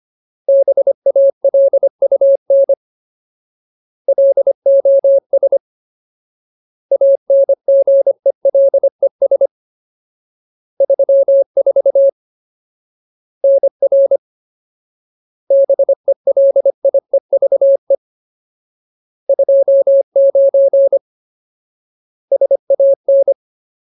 Morse code, heard as a keyed tone, about 25 wpm, 550 Hz. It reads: BALUN LOS ANGELES 34 NR BELIEVE 29 SAN
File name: Ultimate Word List / 25WPM Ultimate